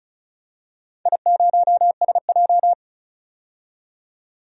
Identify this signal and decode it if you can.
Morse code, heard as a keyed tone, about 36 wpm, 700 Hz. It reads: I0SJ
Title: DXCC-100 Call Signs - 4 Characters - 35wpm